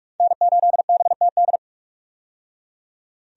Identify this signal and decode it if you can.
Morse code, heard as a keyed tone, about 43 wpm, 700 Hz. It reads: N8BTD
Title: USA Call Signs - 5 Characters - 45wpm